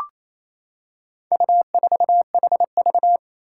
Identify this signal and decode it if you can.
Morse code, heard as a keyed tone, about 28 wpm, 700 Hz. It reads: U4HV
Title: Four Challenging Letter-Number Combinations 28wpm